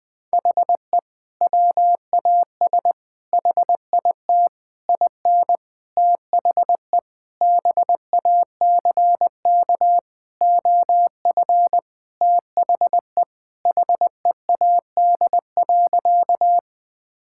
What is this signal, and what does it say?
Morse code, heard as a keyed tone, about 20 wpm, 700 Hz. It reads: HE WAS HIT IN THE BACK OF THE HEAD.
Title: Sentences with 6-Word Phrases - Spoken Beginning - 20wpm